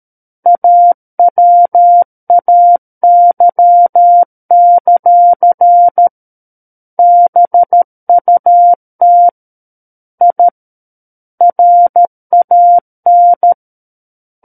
Morse code, heard as a keyed tone, about 13 wpm, 700 Hz. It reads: AWAY; BUT I RAN
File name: Book One / 04-Chapter 4 - 13 WPM